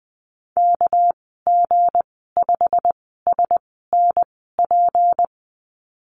Morse code, heard as a keyed tone, about 20 wpm, spaced as for 15 wpm, 700 Hz. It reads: KG5SNP
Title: 231 Lesson 37g - Letter x - Call Signs 20wpm